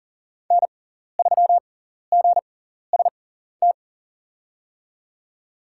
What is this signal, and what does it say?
Morse code, heard as a keyed tone, about 39 wpm, spaced as for 14 wpm, 700 Hz. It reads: N3GST
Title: USA Call Signs - 5 Characters - 6x Character Spacing - 40wpm